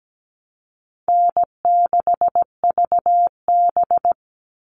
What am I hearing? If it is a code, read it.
Morse code, heard as a keyed tone, about 17 wpm, 700 Hz. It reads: N6VB
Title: All Cut Numbers - 4-Digit 17wpm